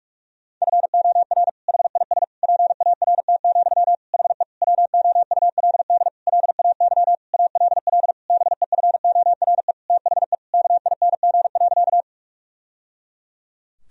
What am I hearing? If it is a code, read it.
Morse code, heard as a keyed tone, about 42 wpm, 700 Hz. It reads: FOR HIS PART, HE WOULD LAY ALL BEFORE THE KING.
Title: Sentences from Top 1000 Words - Review Part 2 - 45wpm